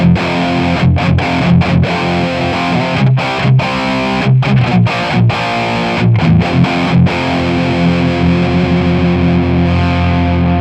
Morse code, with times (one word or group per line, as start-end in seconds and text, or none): none